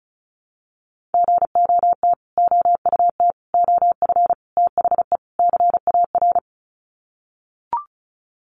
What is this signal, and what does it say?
Morse code, heard as a keyed tone, about 34 wpm, 700 Hz. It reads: GOT OUT OF THE CAR
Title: Sets of 5 Words Encore 35wpm